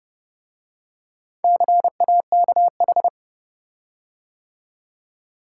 Morse code, heard as a keyed tone, about 29 wpm, 700 Hz. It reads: CAKH